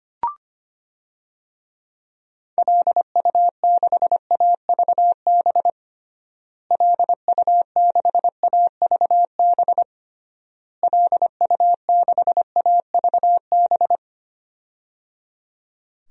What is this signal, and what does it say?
Morse code, heard as a keyed tone, about 25 wpm, 700 Hz. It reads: LU6AVB LU6AVB LU6AVB
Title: DXCC-100 Call Signs - 6 Characters - Repeated 3x - 25wpm